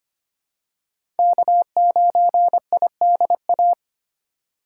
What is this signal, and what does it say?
Morse code, heard as a keyed tone, about 25 wpm, 700 Hz. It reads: K9IDA
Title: USA Call Signs - 5 Characters - 2x Character Spacing - 25wpm